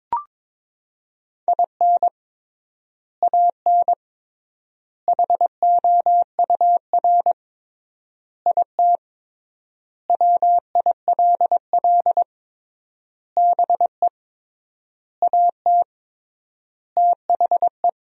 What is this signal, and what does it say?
Morse code, heard as a keyed tone, about 22 wpm, 700 Hz. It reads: IN AN HOUR IT WILL BE AT THE
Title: 3x Word-Spacing - Sentences from Top 400 Words - 22wpm